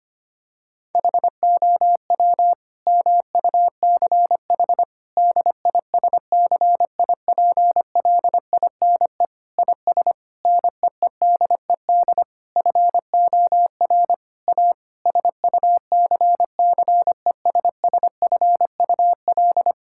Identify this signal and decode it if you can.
Morse code, heard as a keyed tone, about 25 wpm, 700 Hz. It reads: HOW MUCH DISCIPLINE IS NEEDED FOR A SUCCESSFUL
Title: Sentences from Top 1000 Words - Review Part 6 - 25wpm